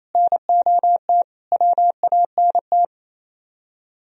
Morse code, heard as a keyed tone, about 34 wpm, 700 Hz. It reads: NOT WANT
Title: Speed-Racing - Sets of 4 Words 28wpm